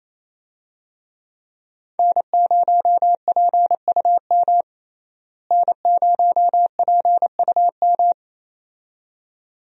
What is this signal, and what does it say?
Morse code, heard as a keyed tone, about 28 wpm, 700 Hz. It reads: N0PUM N0PUM
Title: USA Call Signs - 5 Characters - Repeated 2x - 28wpm